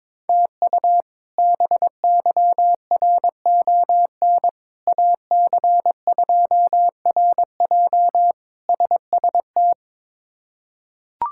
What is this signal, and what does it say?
Morse code, heard as a keyed tone, about 22 wpm, 700 Hz. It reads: TU BYRON AC2RJ SST
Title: K1USN SST QSO Practice - 22wpm